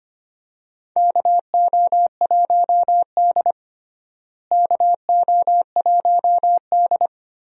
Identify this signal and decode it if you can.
Morse code, heard as a keyed tone, about 25 wpm, 700 Hz. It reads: KO1D KO1D
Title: USA Call Signs - 4 Characters - Repeated 3x - 25wpm